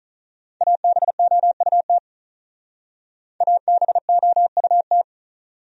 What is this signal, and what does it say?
Morse code, heard as a keyed tone, about 48 wpm, 700 Hz. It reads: ABOUT ABOUT